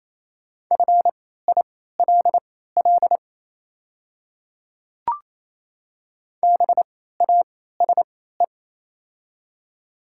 Morse code, heard as a keyed tone, about 28 wpm, spaced as for 16 wpm, 700 Hz. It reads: FILL BASE
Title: 3x Character Spacing - Top 500 Words - 28wpm